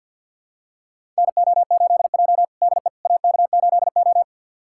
Morse code, heard as a keyed tone, about 50 wpm, 700 Hz. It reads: NO8J DE AK8O
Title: USA Callsign DE Callsign - 4 Characters - 50wpm